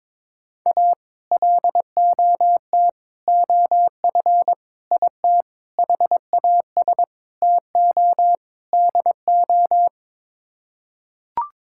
Morse code, heard as a keyed tone, about 22 wpm, 700 Hz. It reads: A LOT OF IT HAS TO DO
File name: practice / 22wpm / Sets of 7 Words - 22wpm